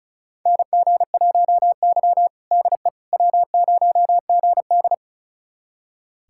Morse code, heard as a keyed tone, about 34 wpm, 700 Hz. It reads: NG1Y DE W0GD
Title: USA Callsign DE Callsign - 4 Characters - 35wpm